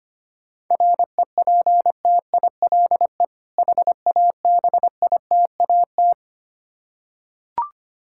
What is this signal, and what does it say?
Morse code, heard as a keyed tone, about 25 wpm, 700 Hz. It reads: REPTILE HABITAT
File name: practice / 25wpm / Difficult Words with Hints - 25wpm